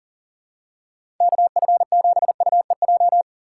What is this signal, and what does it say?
Morse code, heard as a keyed tone, about 38 wpm, 700 Hz. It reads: KF7UEJ
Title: USA Call Signs - 6 Characters - 3x Character Spacing - 40wpm